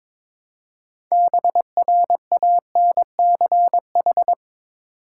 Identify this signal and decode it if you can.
Morse code, heard as a keyed tone, about 22 wpm, 700 Hz. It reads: BRANCH
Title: Six Letter Words - 22wpm